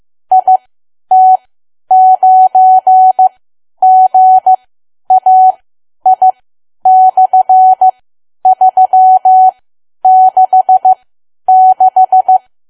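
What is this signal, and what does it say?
Morse code, heard as a keyed tone, about 15 wpm, spaced as for 10 wpm, 750 Hz. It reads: IT9GAI/366
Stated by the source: Morse learning practice at 10 wpm.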